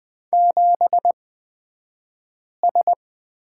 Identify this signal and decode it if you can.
Morse code, heard as a keyed tone, about 20 wpm, spaced as for 5 wpm, 700 Hz. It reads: 7S
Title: 242 Lesson 39d - Letter z - Two Characters 20wpm